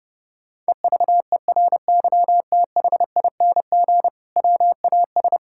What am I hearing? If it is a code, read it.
Morse code, heard as a keyed tone, about 30 wpm, 700 Hz. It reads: EVERYTHING WAS
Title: Sentences from Top 1600 Words 30wpm